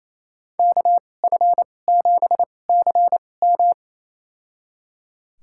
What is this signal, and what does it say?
Morse code, heard as a keyed tone, about 28 wpm, spaced as for 20 wpm, 700 Hz. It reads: KF7CM
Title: USA Call Signs - 5 Characters - 2x Character Spacing - 28wpm